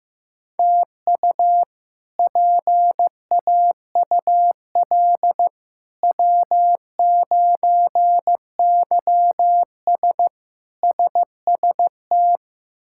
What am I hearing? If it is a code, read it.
Morse code, heard as a keyed tone, about 15 wpm, 700 Hz. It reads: TU PAUL W9YS SST
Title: K1USN SST QSO Practice - 15wpm